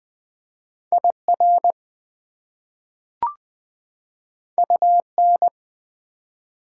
Morse code, heard as a keyed tone, about 20 wpm, 700 Hz. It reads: IR UN
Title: Prefixes and Suffixes 20wpm